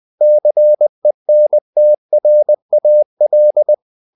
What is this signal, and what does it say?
Morse code, heard as a keyed tone, about 20 wpm, 600 Hz. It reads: CENTRAL
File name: CW / Practice / NationalCapitals20